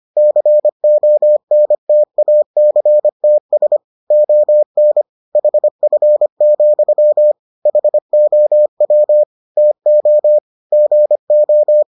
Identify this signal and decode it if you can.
Morse code, heard as a keyed tone, about 25 wpm, 600 Hz. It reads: CONTACTS ON HF, HOW TO GO